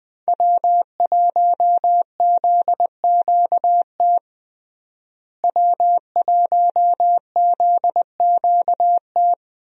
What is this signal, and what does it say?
Morse code, heard as a keyed tone, about 21 wpm, 700 Hz. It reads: W1ZQT W1ZQT
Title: USA Call Signs - 5 Characters - Repeated 2x - 20wpm